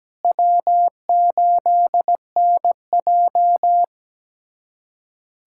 Morse code, heard as a keyed tone, about 17 wpm, 700 Hz. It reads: W8NJ
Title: USA Callsign DE Callsign - 4 Characters - 17wpm